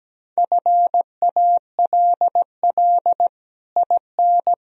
Morse code, heard as a keyed tone, about 17 wpm, 700 Hz. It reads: FALL IN
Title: Sentences from Top 1000 Words - Review Part 2 - 17wpm